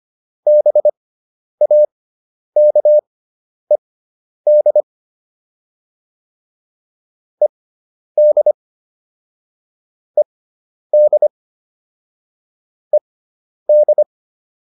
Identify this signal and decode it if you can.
Morse code, heard as a keyed tone, about 25 wpm, spaced as for 10 wpm, 600 Hz. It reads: BAKED ED ED ED
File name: ED5 10